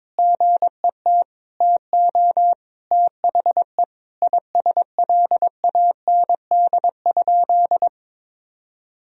Morse code, heard as a keyed tone, about 22 wpm, 700 Hz. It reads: GET TO THE ISLAND?
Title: Sentences from Top 1000 Words - Review Part 3 - 22wpm